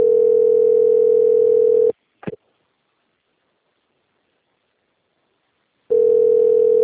Ringback tone (North America), 440 + 480 Hz.
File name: out- - - - - .23252